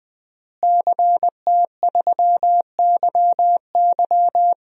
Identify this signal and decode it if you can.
Morse code, heard as a keyed tone, about 20 wpm, 700 Hz. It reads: CT3YY
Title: DXCC-100 Call Signs - 5 Characters - Repeated 3x - 20wpm